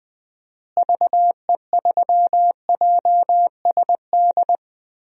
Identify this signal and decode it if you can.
Morse code, heard as a keyed tone, about 20 wpm, 700 Hz. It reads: VE3JSD
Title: DXCC-100 Call Signs - 6 Characters - Repeated 3x - 20wpm